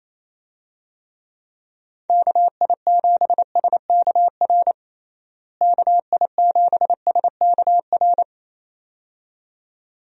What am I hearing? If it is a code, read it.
Morse code, heard as a keyed tone, about 27 wpm, 700 Hz. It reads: KI7SKR KI7SKR
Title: USA Call Signs - 6 Characters - Repeated 2x - 28wpm